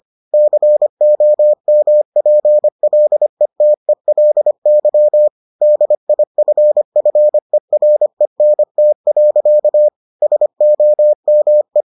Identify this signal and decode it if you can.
Morse code, heard as a keyed tone, about 25 wpm, 600 Hz. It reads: COMPLETELY DIFFERENT. SOME